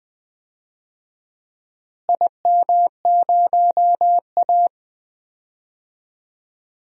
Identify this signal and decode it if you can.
Morse code, heard as a keyed tone, about 20 wpm, 700 Hz. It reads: IM0A
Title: DXCC-100 Call Signs - 4 Characters - Repeated 3x - 20wpm